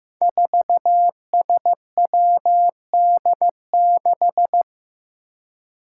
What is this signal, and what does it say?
Morse code, heard as a keyed tone, about 15 wpm, 700 Hz. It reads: 4SWD6
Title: Five Challenging Letter-Number Combinations 15wpm